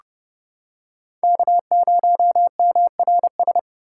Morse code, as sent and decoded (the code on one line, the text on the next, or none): -.- ----- -- .-. ...
K0MRS